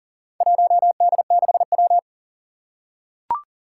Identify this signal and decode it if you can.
Morse code, heard as a keyed tone, about 39 wpm, 700 Hz. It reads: 1D6W
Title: Four Challenging Letter-Number Combinations 40wpm